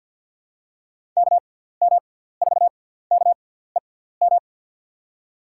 Morse code, heard as a keyed tone, about 47 wpm, spaced as for 17 wpm, 700 Hz. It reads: KM4KEM
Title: USA Call Signs - 6 Characters - 6x Character Spacing - 50wpm